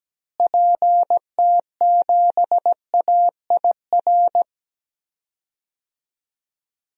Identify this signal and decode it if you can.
Morse code, heard as a keyed tone, about 17 wpm, 700 Hz. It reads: PT7AIR
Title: DXCC-100 Call Signs - 6 Characters - Repeated 2x - 17wpm